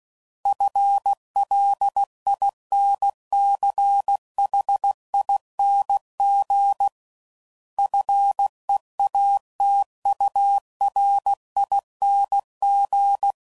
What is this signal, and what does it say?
Morse code, heard as a keyed tone, about 16 wpm, 800 Hz. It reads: FLINCHING FEATURING
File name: Suffixes / Special INGs / Sngl Ings